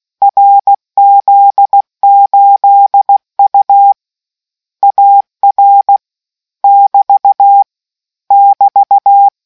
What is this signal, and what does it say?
Morse code, heard as a keyed tone, about 16 wpm, 800 Hz. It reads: RZ8U AR <BT> <BT>